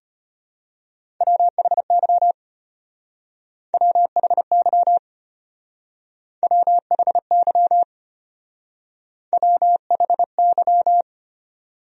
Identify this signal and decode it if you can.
Morse code, heard as a keyed tone, about 31 wpm, 700 Hz. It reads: WHY WHY WHY WHY
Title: Speed-Racing Top 300 Words 25wpm